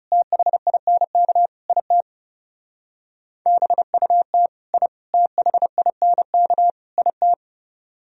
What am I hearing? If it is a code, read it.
Morse code, heard as a keyed tone, about 37 wpm, 690 Hz. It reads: THINK IT BUT I THINK IT
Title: Speed-Racing - Sets of 4 Words 30wpm